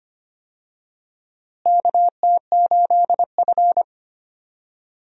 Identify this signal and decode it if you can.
Morse code, heard as a keyed tone, about 25 wpm, 700 Hz. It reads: KT8F
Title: USA Call Signs - 4 Characters - Repeated 3x - 25wpm